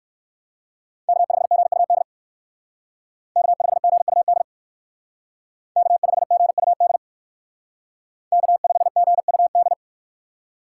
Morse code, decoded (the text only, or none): K5GUD K5GUD K5GUD K5GUD